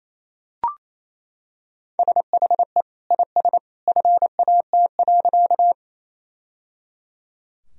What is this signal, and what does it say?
Morse code, heard as a keyed tone, about 28 wpm, 700 Hz. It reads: SHE IS FAT.
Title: Sentences from Top 1000 Words - Review Part 4 - 28wpm